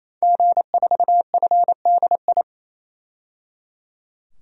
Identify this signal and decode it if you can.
Morse code, heard as a keyed tone, about 28 wpm, 700 Hz. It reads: G4FDI
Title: DXCC-100 Call Signs - 5 Characters - 28wpm